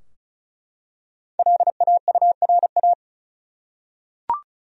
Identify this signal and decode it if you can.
Morse code, heard as a keyed tone, about 36 wpm, 700 Hz. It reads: LAURA
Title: USA Common Names - Max 5 Letters - 35wpm